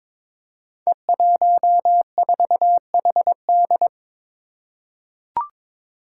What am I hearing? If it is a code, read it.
Morse code, heard as a keyed tone, about 22 wpm, 700 Hz. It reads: E14HD
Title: Five Challenging Letter-Number Combinations 22wpm